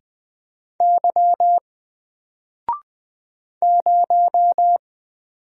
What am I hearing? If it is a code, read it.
Morse code, heard as a keyed tone, about 20 wpm, 700 Hz. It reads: Y 0
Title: Single Letter-Number Alt 20wpm